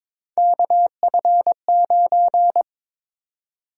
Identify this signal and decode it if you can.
Morse code, heard as a keyed tone, about 22 wpm, 700 Hz. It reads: KF9
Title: USA Call Sign Prefixes - Repeated 2x - 22wpm